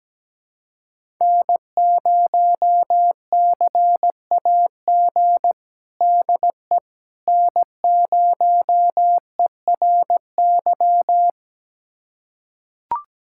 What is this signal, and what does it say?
Morse code, heard as a keyed tone, about 17 wpm, 700 Hz. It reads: N0CAG DE N0ERY
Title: USA Callsign DE Callsign - 5 Characters - 17wpm